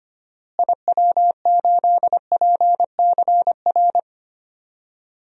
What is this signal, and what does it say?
Morse code, heard as a keyed tone, about 25 wpm, 700 Hz. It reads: IW8PCR
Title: DXCC-100 Call Signs - 6 Characters - Repeated 3x - 25wpm